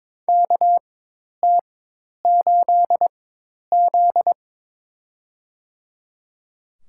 Morse code, heard as a keyed tone, about 23 wpm, spaced as for 10 wpm, 700 Hz. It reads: KT8Z
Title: USA Call Signs - 4 Characters - 4x Character Spacing - 22wpm